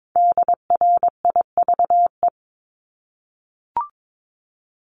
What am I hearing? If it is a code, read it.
Morse code, heard as a keyed tone, about 22 wpm, 700 Hz. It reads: DRIVE
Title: Top 500 Words Repeated 22wpm